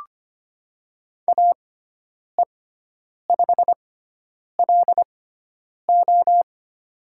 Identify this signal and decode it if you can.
Morse code, heard as a keyed tone, about 25 wpm, spaced as for 9 wpm, 700 Hz. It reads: AE5LO
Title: USA Call Signs - 6x Character Spacing - 25wpm